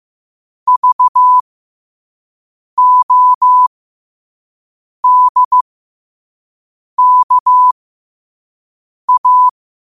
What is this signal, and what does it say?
Morse code, heard as a keyed tone, about 15 wpm, spaced as for 5 wpm, 1000 Hz. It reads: VODKA
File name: 5wpm